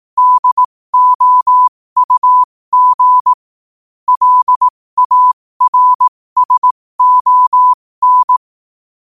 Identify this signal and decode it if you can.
Morse code, heard as a keyed tone, about 18 wpm, spaced as for 15 wpm, 1000 Hz. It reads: DOUG LARSON
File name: qotdfiles / 15wpm